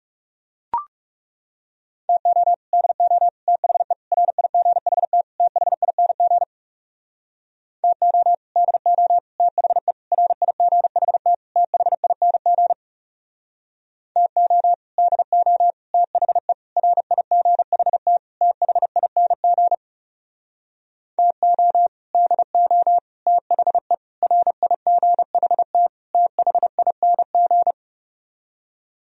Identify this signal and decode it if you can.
Morse code, heard as a keyed tone, about 40 wpm, 700 Hz. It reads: TO DO THE RIGHT THING TO DO THE RIGHT THING TO DO THE RIGHT THING TO DO THE RIGHT THING